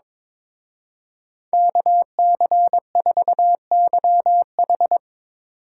Morse code, heard as a keyed tone, about 22 wpm, 700 Hz. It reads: KC4YH
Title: Speed Racing - USA 2x2 Call Signs 22wpm